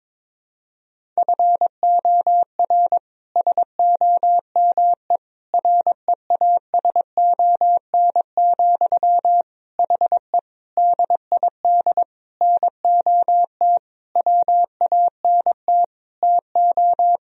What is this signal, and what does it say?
Morse code, heard as a keyed tone, about 22 wpm, 700 Hz. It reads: FOR SOME REASON, HE DID NOT WANT TO
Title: Sentences from Top 1000 Words - Review Part 3 - 22wpm